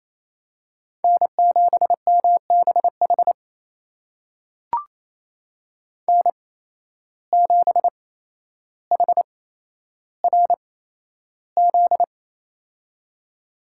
Morse code, heard as a keyed tone, about 28 wpm, spaced as for 12 wpm, 700 Hz. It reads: N7MBH N7HRZ
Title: USA Call Signs - 5 Characters - 8x Character Spacing - 28wpm